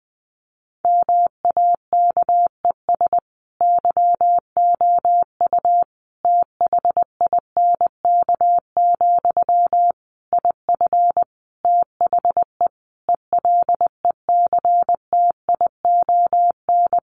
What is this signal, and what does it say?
Morse code, heard as a keyed tone, about 20 wpm, 700 Hz. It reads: MAKES YOU THINK, IF THE ELECTION
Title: A Sound Of Thunder 20wpm